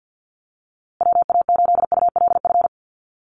Morse code, heard as a keyed tone, about 50 wpm, 700 Hz. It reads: WA8ULF
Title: USA Call Signs Repeated 2x 50wpm